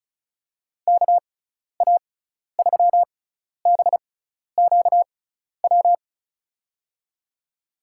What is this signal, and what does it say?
Morse code, heard as a keyed tone, about 34 wpm, spaced as for 12 wpm, 700 Hz. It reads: KA3BQW
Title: USA Call Signs - 6 Characters - 6x Character Spacing - 35wpm